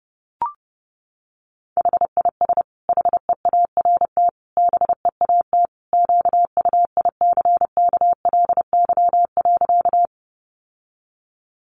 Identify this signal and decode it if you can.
Morse code, heard as a keyed tone, about 30 wpm, 700 Hz. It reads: HIS HEART BEAT QUICKLY.
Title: Sentences from Top 1300 Words 30wpm